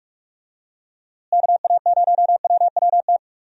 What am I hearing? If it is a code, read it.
Morse code, heard as a keyed tone, about 42 wpm, 700 Hz. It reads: KA0WWT
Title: USA Call Signs - 6 Characters - Repeated 3x - 45wpm